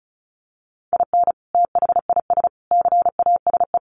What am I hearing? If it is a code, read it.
Morse code, heard as a keyed tone, about 34 wpm, 700 Hz. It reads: IN THIS CASE
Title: Sets of 3 Words 35wpm